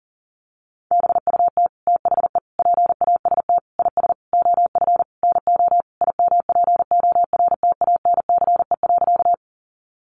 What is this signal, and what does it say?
Morse code, heard as a keyed tone, about 40 wpm, 700 Hz. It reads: BUT THE PAST IS OF NO IMPORTANCE.
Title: Sentences from Top 1200 Words 40wpm